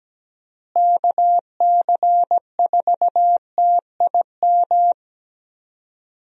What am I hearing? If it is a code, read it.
Morse code, heard as a keyed tone, about 17 wpm, 700 Hz. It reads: KC4TIM
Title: USA Call Signs - 6 Characters - Repeated 3x - 17wpm